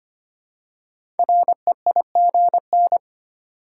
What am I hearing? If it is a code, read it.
Morse code, heard as a keyed tone, about 25 wpm, 700 Hz. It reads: REIGN
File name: 5 Letter Words - First 3 Letters Given - 25wpm